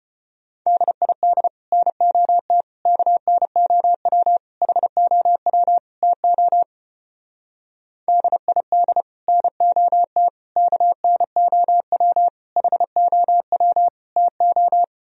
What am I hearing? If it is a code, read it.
Morse code, heard as a keyed tone, about 31 wpm, 700 Hz. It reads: DID NOT KNOW HOW TO DID NOT KNOW HOW TO
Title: Speed-Racing - Sets of 5 Words 25wpm